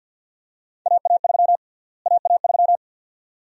Morse code, heard as a keyed tone, about 46 wpm, 700 Hz. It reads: AA3 AA3
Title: USA Call Sign Prefixes - Repeated 2x - 50wpm